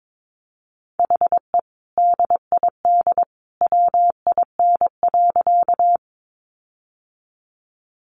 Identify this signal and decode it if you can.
Morse code, heard as a keyed tone, about 22 wpm, 700 Hz. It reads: HE DID WIN.
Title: Sentences from Top 1500 Words 22wpm